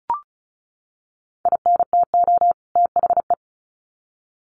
Morse code, heard as a keyed tone, about 34 wpm, 700 Hz. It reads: INTO THE
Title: Sets of 2 Words 35wpm